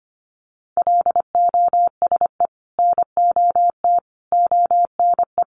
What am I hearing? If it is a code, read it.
Morse code, heard as a keyed tone, about 25 wpm, 700 Hz. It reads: LOSE NOT ONE
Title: Sentences from Top 1100 Words 25wpm